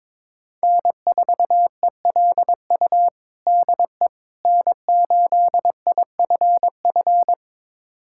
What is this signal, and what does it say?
Morse code, heard as a keyed tone, about 22 wpm, 700 Hz. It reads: N4ELU DE N8IFF
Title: USA Callsign DE Callsign - 5 Characters - 22wpm